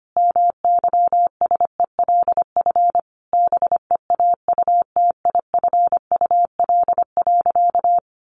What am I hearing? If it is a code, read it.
Morse code, heard as a keyed tone, about 25 wpm, 700 Hz. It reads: MYSELF BEAUTIFUL.
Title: Sentences from Top 1300 Words 25wpm